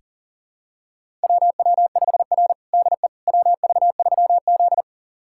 Morse code, heard as a keyed tone, about 40 wpm, 700 Hz. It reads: WW5R DE WV3Z
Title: USA Callsign DE Callsign - 4 Characters - 40wpm